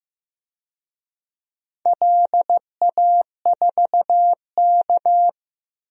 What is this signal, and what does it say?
Morse code, heard as a keyed tone, about 16 wpm, 700 Hz. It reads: LA4K
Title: DXCC-100 Call Signs - 4 Characters - Repeated 3x - 15wpm